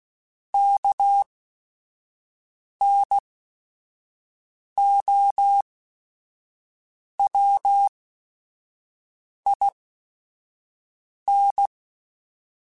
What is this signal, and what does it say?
Morse code, heard as a keyed tone, about 16 wpm, spaced as for 5 wpm, 800 Hz. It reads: KNOWIN